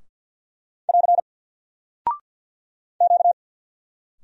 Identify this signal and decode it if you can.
Morse code, heard as a keyed tone, about 50 wpm, 700 Hz. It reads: <AR> Q
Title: Single Character (Letter:Number:Punctuation:Prosign) Alt 50wpm